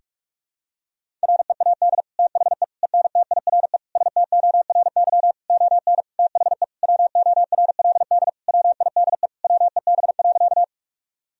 Morse code, heard as a keyed tone, about 43 wpm, 700 Hz. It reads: READ THE ENTIRE STORY ON THE WORLD WIDE WEB.
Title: Sentences from Top 1000 Words - Review Part 4 - 45wpm